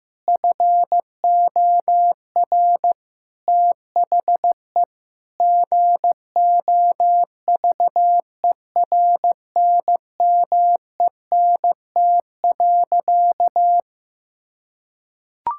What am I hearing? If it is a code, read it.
Morse code, heard as a keyed tone, about 15 wpm, 700 Hz. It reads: FOR THE GOVERNMENT.
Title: Sentences from Top 1000 Words - Review Part 6 - 15wpm